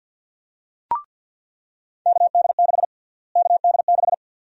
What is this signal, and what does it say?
Morse code, heard as a keyed tone, about 50 wpm, 700 Hz. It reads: KD6 KD6
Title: USA Call Sign Prefixes - Repeated 2x - 50wpm